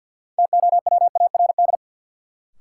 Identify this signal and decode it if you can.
Morse code, heard as a keyed tone, about 47 wpm, 700 Hz. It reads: TOWARD
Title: Sentences from Top 1000 Words - Review Part 6 - New Word Drill - 50wpm